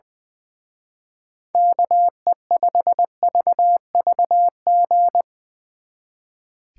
Morse code, heard as a keyed tone, about 22 wpm, 700 Hz. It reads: KE5VVG
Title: Speed Racing - USA 2x3 Call Signs 20wpm